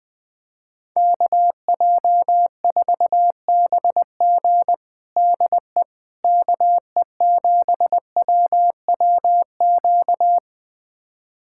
Morse code, heard as a keyed tone, about 20 wpm, 700 Hz. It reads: KJ4BG DE KE7WWQ
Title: USA Callsign DE Callsign 20wpm